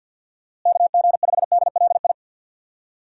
Morse code, heard as a keyed tone, about 51 wpm, 700 Hz. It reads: KG5DLI
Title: USA Call Signs - 6 Characters - Repeated 3x - 50wpm